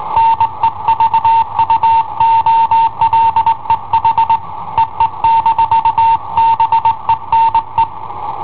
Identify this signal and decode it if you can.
Morse code, heard as a keyed tone, about 20 wpm, 910 Hz. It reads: NE VUOLE HEE-BENE
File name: i7ohp / cassetta i7ohp B